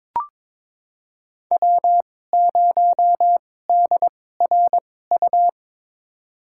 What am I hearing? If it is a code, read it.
Morse code, heard as a keyed tone, about 22 wpm, spaced as for 16 wpm, 700 Hz. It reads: W0DRU
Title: USA Call Signs - 5 Characters - 2x Character Spacing - 22wpm